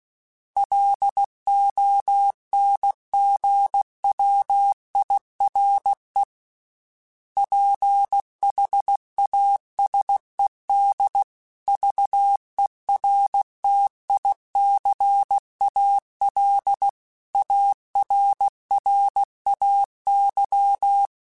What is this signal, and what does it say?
Morse code, heard as a keyed tone, about 16 wpm, 800 Hz. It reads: LONG WIRE PHASED VERTICAL ARRAY